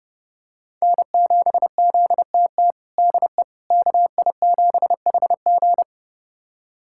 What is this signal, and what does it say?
Morse code, heard as a keyed tone, about 30 wpm, 700 Hz. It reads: N7ZTT DE KI7HG
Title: USA Callsign DE Callsign - 5 Characters - 30wpm